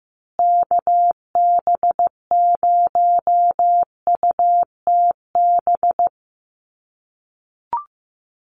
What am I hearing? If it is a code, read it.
Morse code, heard as a keyed tone, about 15 wpm, 700 Hz. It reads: KB0UTB
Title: USA Call Signs Repeated 3x 15wpm